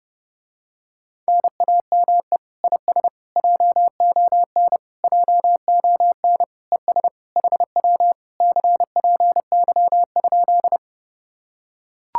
Morse code, heard as a keyed tone, about 29 wpm, 700 Hz. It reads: NAME IS JON JON ES HW CPY?